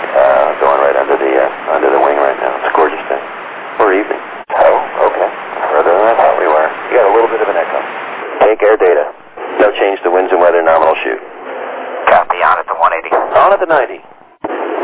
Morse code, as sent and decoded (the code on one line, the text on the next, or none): none